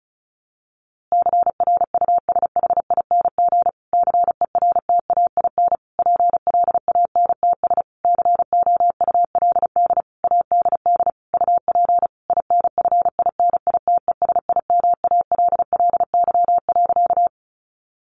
Morse code, heard as a keyed tone, about 34 wpm, 700 Hz. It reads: CRUSHING CERTAIN PLANTS COULD ADD UP INFINITESIMALLY.
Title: A Sound Of Thunder 35wpm